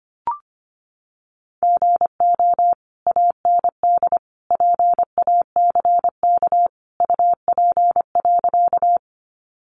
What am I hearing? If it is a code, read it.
Morse code, heard as a keyed tone, about 25 wpm, 700 Hz. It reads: GO AND PACK UP.
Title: Sentences from Top 1300 Words 25wpm